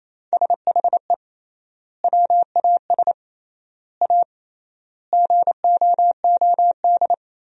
Morse code, heard as a keyed tone, about 28 wpm, 700 Hz. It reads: SHE WAS A GOOD
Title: 3x Word-Spacing - Sentences from Top 400 Words - 28wpm